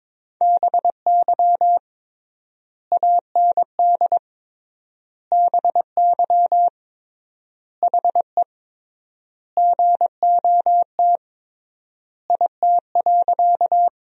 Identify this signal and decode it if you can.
Morse code, heard as a keyed tone, about 22 wpm, 700 Hz. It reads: BY AND BY HE GOT IT.
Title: 3x Word-Spacing - Sentences from Top 300 Words - 22wpm